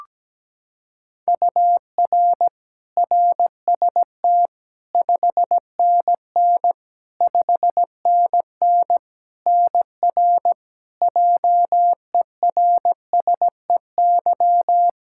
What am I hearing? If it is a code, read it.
Morse code, heard as a keyed tone, about 17 wpm, 700 Hz. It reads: UR RST 5NN 5NN NR JERSEY
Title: Practice QSO - Exchange 1 - Beginning - 17wpm